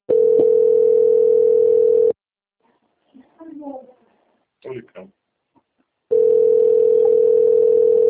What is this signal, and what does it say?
Ringback tone (North America), 440 + 480 Hz, with speech.
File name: out- - - - - .9238